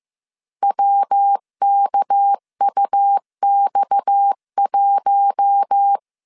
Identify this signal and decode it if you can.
Morse code, heard as a keyed tone, about 15 wpm, 800 Hz. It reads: WKUX1
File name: CW / DET / 1 6 1500C S C 14wpm 2951